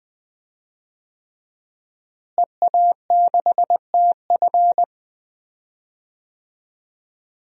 Morse code, heard as a keyed tone, about 20 wpm, 700 Hz. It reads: EA6TF
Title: DXCC-100 Call Signs - 5 Characters - Repeated 3x - 20wpm